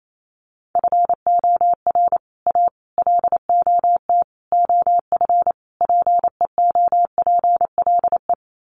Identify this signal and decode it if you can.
Morse code, heard as a keyed tone, about 28 wpm, 700 Hz. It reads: FOR A LOT OF PEOPLE
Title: Sets of 5 Words Encore 28wpm